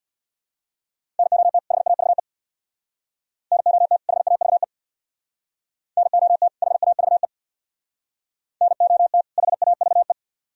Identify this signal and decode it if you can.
Morse code, heard as a keyed tone, about 61 wpm, 700 Hz. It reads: NOT HAVE NOT HAVE NOT HAVE NOT HAVE
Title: Speed-Racing - Sets of 2 Words 50wpm